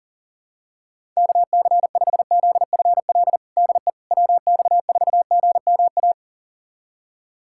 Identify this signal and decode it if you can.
Morse code, heard as a keyed tone, about 40 wpm, 700 Hz. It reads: KC5ZFL DE WX4GMA
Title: USA Callsign DE Callsign - 6 Characters - 40wpm